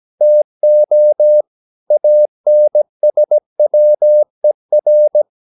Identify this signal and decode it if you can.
Morse code, heard as a keyed tone, about 17 wpm, 600 Hz. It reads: TO ANSWER
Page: Romanen Never af Ken Follet på engelsk.